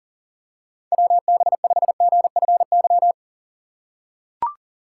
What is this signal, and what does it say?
Morse code, heard as a keyed tone, about 40 wpm, 700 Hz. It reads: WB5GFY
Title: USA Call Signs - 6 Characters - 6x Character Spacing - 40wpm